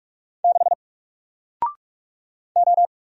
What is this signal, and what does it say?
Morse code, heard as a keyed tone, about 44 wpm, 700 Hz. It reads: 6 O
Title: Single Character (Letter:Number:Punctuation:Prosign) Alt 45wpm